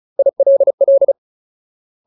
Morse code, heard as a keyed tone, about 34 wpm, 550 Hz. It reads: ILL
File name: Ultimate Word List / 35WPM Ultimate